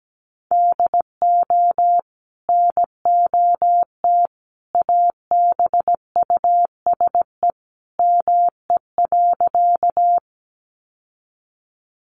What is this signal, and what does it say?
Morse code, heard as a keyed tone, about 17 wpm, 700 Hz. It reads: DO NOT ABUSE ME.
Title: Sentences from Top 1300 Words 17wpm